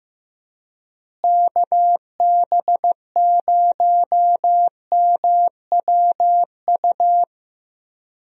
Morse code, heard as a keyed tone, about 15 wpm, 700 Hz. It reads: KB0MWU
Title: USA Call Signs - 6 Characters - Repeated 3x - 15wpm